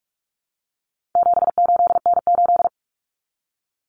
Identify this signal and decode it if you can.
Morse code, heard as a keyed tone, about 44 wpm, 700 Hz. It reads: 78N8
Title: Common Cut Numbers - 4-Digit 45wpm